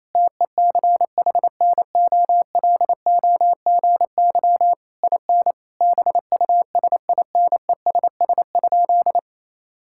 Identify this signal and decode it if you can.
Morse code, heard as a keyed tone, about 27 wpm, 700 Hz. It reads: TECHNOLOGY IN BUSINESS?
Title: Sentences from Top 1000 Words - Review Part 5 - 28wpm